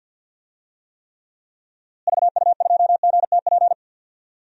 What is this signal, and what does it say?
Morse code, heard as a keyed tone, about 46 wpm, 700 Hz. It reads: VU2GTP
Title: DXCC-100 Call Signs - 6 Characters - Repeated 3x - 50wpm